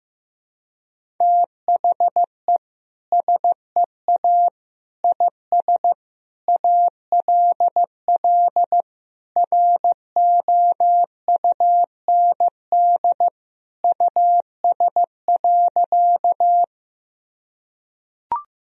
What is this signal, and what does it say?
Morse code, heard as a keyed tone, about 16 wpm, 700 Hz. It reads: THE SEA IS ALL ROUND US.
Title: Sentences from Top 1000 Words - Review Part 1 - 15wpm